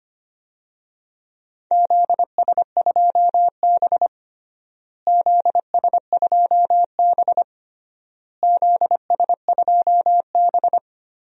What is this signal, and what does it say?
Morse code, heard as a keyed tone, about 25 wpm, 700 Hz. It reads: ZS2B ZS2B ZS2B
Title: DXCC-100 Call Signs - 4 Characters - Repeated 3x - 25wpm